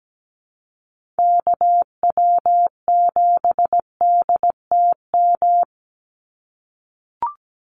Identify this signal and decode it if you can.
Morse code, heard as a keyed tone, about 17 wpm, 700 Hz. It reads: KW7DTM
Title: USA Call Signs 17wpm